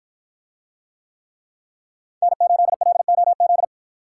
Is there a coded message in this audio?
Yes: Morse code from 2.2 to 3.6 s, heard as a keyed tone, about 53 wpm, 700 Hz. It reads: N8LOZ